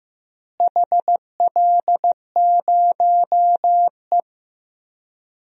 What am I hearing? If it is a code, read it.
Morse code, heard as a keyed tone, about 16 wpm, 700 Hz. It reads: HL0E
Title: DXCC-100 Call Signs - 4 Characters - Repeated 2x - 15wpm